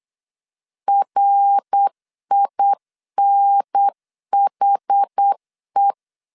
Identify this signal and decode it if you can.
Morse code, heard as a keyed tone, about 8 wpm, 800 Hz. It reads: RINHE